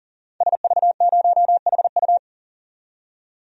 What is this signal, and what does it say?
Morse code, heard as a keyed tone, about 40 wpm, 700 Hz. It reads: SV0HU
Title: DXCC-100 Call Signs - 5 Characters - 40wpm